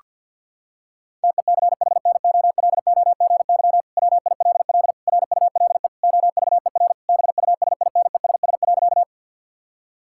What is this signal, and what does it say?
Morse code, heard as a keyed tone, about 50 wpm, 700 Hz. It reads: TECHNOLOGY WILL RULE OVER BUSINESS.
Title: Sentences from Top 1000 Words - Review Part 5 - 50wpm